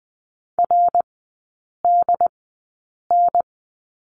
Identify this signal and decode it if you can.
Morse code, heard as a keyed tone, about 20 wpm, spaced as for 8 wpm, 700 Hz. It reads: RDN
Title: 133 Lesson 22e - Question Mark - Three Characters 20wpm